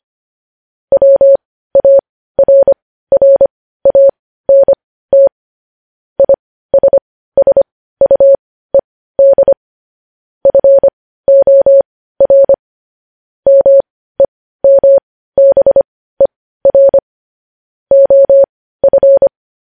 Morse code, heard as a keyed tone, about 25 wpm, spaced as for 15 wpm, 550 Hz. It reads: WARRANT ISSUED FOR MEMBER OF
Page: news headlines in morse code at 15 wpm